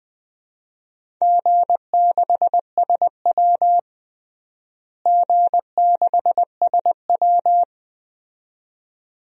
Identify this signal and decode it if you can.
Morse code, heard as a keyed tone, about 20 wpm, 700 Hz. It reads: G6SW G6SW
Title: DXCC-100 Call Signs - 4 Characters - Repeated 2x - 20wpm